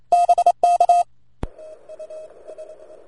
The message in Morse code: BK

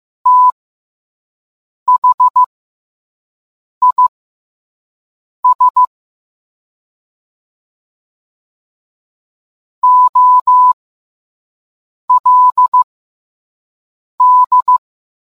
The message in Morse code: THIS OLD

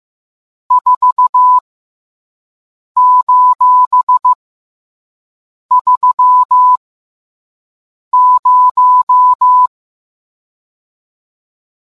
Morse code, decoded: 4:30

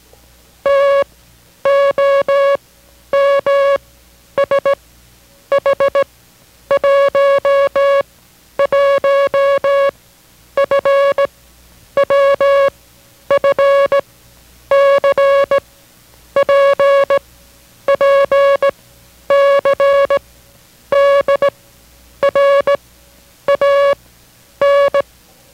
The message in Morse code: T O M S H 1 1 F W F C P P C D R A N

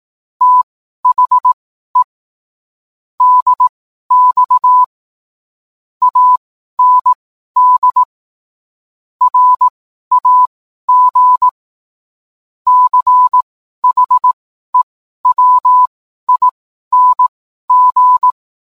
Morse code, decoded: THE DX AND RAG CHEWING

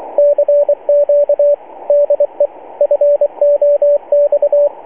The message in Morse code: CQ DE FOX

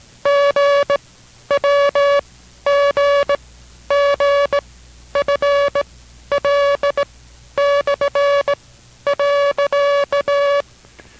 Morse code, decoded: GWGGFL/.

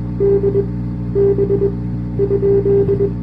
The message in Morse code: D B ?

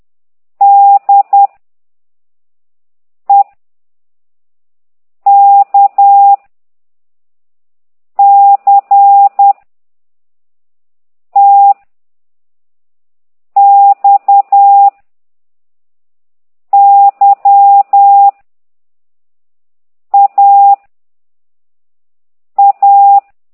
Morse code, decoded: D E K C T X Y A A

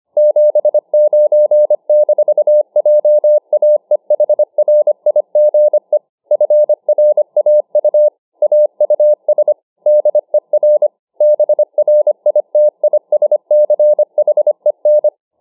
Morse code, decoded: 79-JAEHRIGE FRAU AUS DER BRITISCHEN

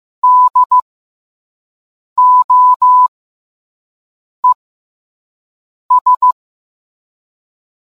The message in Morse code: DOES

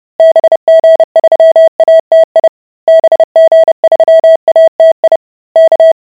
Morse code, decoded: BG3ATI BG3ATI K

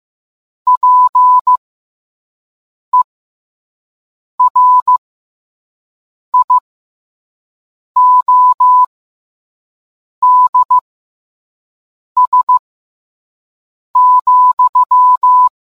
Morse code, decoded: PERIODS,

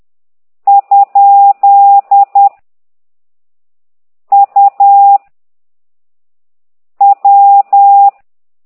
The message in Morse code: ? U W

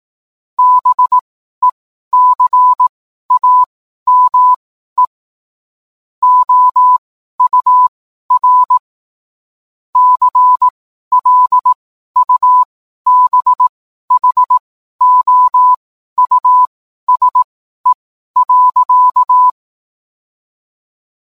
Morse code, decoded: BECAME OUR CLUBHOUSE.

